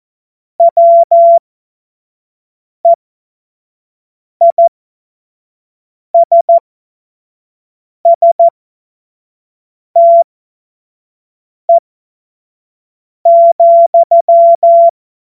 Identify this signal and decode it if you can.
Morse code, heard as a keyed tone, about 14 wpm, spaced as for 5 wpm, 680 Hz. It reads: WEISSTE,
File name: Morsekurs / Witze / Witz 30. Dezember 2025 25 F70